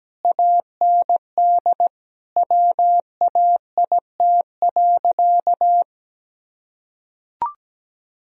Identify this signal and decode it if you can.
Morse code, heard as a keyed tone, about 17 wpm, 700 Hz. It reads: AND WAIT.
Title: Sentences from Top 1000 Words - Review Part 4 - 17wpm